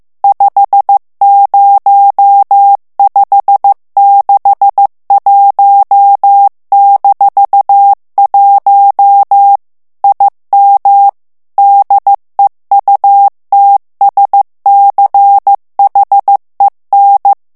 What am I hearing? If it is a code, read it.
Morse code, heard as a keyed tone, about 15 wpm, 800 Hz. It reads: 50561-1 IM DEUTSCHEN
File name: morse mp3 / ALLEMAND texte clair 1 vitesse 14 mots minute